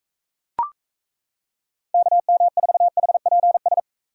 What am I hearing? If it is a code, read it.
Morse code, heard as a keyed tone, about 34 wpm, 700 Hz. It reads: KM4HPS
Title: Speed Racing - USA 2x3 Call Signs 28wpm